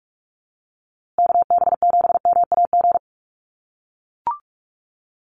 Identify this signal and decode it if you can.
Morse code, heard as a keyed tone, about 44 wpm, 700 Hz. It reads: KB7MAG
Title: USA Call Signs Repeated 3x 45wpm